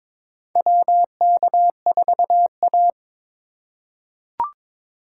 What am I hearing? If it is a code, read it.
Morse code, heard as a keyed tone, about 22 wpm, 700 Hz. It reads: WK4A
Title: USA Call Signs - 4 Characters - 8x Character Spacing - 22wpm